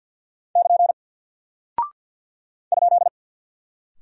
Morse code, heard as a keyed tone, about 48 wpm, 700 Hz. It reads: <KN> ?
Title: Single Character (Letter:Number:Punctuation:Prosign) Alt 50wpm